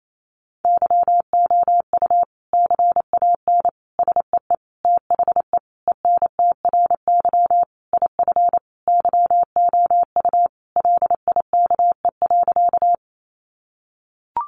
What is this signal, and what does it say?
Morse code, heard as a keyed tone, about 28 wpm, 700 Hz. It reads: YOU CAN SEE THE ENTRY IF YOU LIKE.
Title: Sentences from Top 1500 Words 28wpm